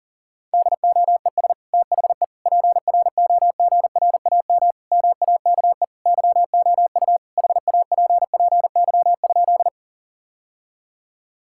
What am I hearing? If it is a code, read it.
Morse code, heard as a keyed tone, about 37 wpm, 700 Hz. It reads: DOES THE PROGRAM MAKE YOU HAPPY?
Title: Sentences from Top 1000 Words - Review Part 5 - 40wpm